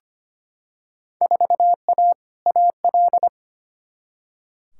Morse code, heard as a keyed tone, about 25 wpm, 700 Hz. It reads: 4A AL
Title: Field Day Exchanges - 25wpm